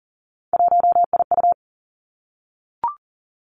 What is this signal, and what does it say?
Morse code, heard as a keyed tone, about 38 wpm, 700 Hz. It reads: 1IU
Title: Three Challenging Letter-Number Combinations 40wpm